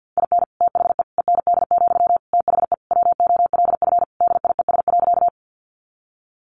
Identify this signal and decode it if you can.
Morse code, heard as a keyed tone, about 50 wpm, 700 Hz. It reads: IN THE END, THE WOLF DIES.
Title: Additional Easy Phrases 50wpm